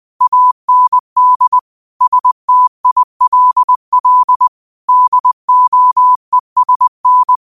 Morse code, heard as a keyed tone, about 20 wpm, 1000 Hz. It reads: AND STILL DOESN